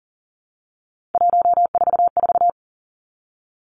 Morse code, heard as a keyed tone, about 40 wpm, 700 Hz. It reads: 144
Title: Three Challenging Letter-Number Combinations 40wpm